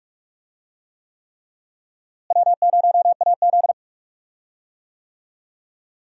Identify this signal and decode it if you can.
Morse code, heard as a keyed tone, about 46 wpm, 700 Hz. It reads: W0AZ
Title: USA Call Signs - 4 Characters - Repeated 3x - 45wpm